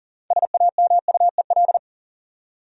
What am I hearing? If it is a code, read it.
Morse code, heard as a keyed tone, about 40 wpm, 700 Hz. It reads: SAMUEL
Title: USA Common Names - 40wpm